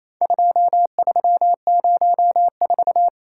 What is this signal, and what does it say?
Morse code, heard as a keyed tone, about 28 wpm, 700 Hz. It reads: 2304
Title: Worked All Europe DX Contest - QTC Exchange - 28wpm